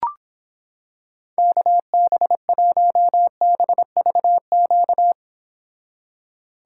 Morse code, heard as a keyed tone, about 26 wpm, 700 Hz. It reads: KB1BVQ